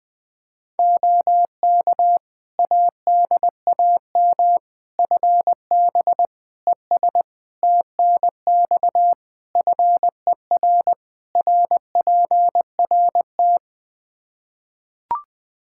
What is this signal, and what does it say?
Morse code, heard as a keyed tone, about 21 wpm, 700 Hz. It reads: OK ADAM FB ES TNX FER RPRT
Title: Practice QSO - Exchange 2 - Info - 20wpm